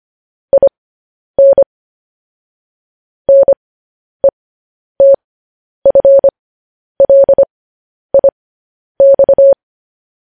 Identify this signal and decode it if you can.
Morse code, heard as a keyed tone, about 25 wpm, spaced as for 10 wpm, 550 Hz. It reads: IN NETFLIX
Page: news headlines in morse code at 10 wpm